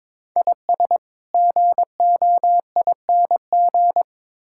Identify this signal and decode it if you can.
Morse code, heard as a keyed tone, about 21 wpm, 700 Hz. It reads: IS GOING
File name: practice / 22wpm / Build A Phrase - 5-Word Phrases - 22wpm